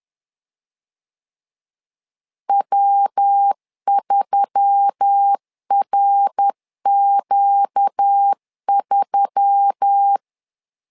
Morse code, heard as a keyed tone, about 10 wpm, 800 Hz. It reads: W3RQ3
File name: LEZIONI / 4 Q1S3 10wpm 2427